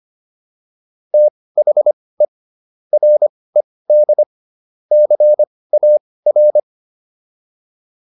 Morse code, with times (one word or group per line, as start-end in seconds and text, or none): 1.1-2.2 THE
2.9-4.2 RED
4.9-6.6 CAR